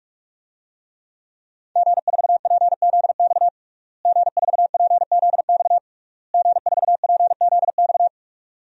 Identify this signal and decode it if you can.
Morse code, heard as a keyed tone, about 46 wpm, 700 Hz. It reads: G4PZX G4PZX G4PZX
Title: DXCC-100 Call Signs - 5 Characters - Repeated 3x - 45wpm